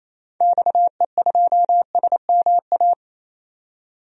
Morse code, heard as a keyed tone, about 28 wpm, 700 Hz. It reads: XE2SMA
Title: DXCC-100 Call Signs - 6 Characters - 28wpm